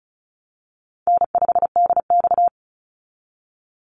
Morse code, heard as a keyed tone, about 34 wpm, 700 Hz. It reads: N5DX
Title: Call Signs Repeated 3x - 35wpm